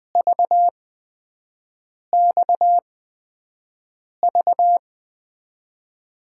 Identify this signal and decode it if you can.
Morse code, heard as a keyed tone, about 20 wpm, spaced as for 5 wpm, 700 Hz. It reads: VXV
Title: USA Call Signs - 5 Characters - 8x Character Spacing - 20wpm